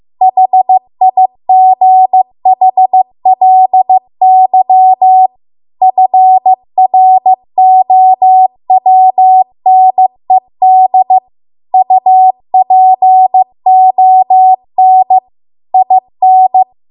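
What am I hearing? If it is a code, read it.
Morse code, heard as a keyed tone, about 15 wpm, 750 Hz. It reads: HIGHLY FROWNED UPON IN
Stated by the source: Morse code practice at 15 wpm from the national amateur radio association's code archive.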